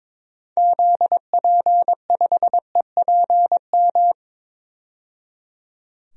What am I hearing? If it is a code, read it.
Morse code, heard as a keyed tone, about 22 wpm, 700 Hz. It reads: ZP5EPM
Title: DXCC-100 Call Signs - 6 Characters - 22wpm